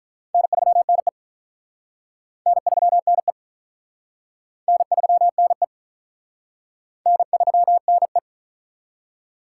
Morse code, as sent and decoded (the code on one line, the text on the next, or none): -. ...-- -. / -. ...-- -. / -. ...-- -. / -. ...-- -. .
N3NE N3NE N3NE N3NE